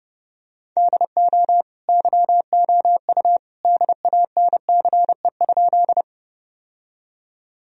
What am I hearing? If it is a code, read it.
Morse code, heard as a keyed tone, about 30 wpm, 700 Hz. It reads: DO YOU DANCE?
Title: Sentences from Top 1000 Words - Review Part 4 - 30wpm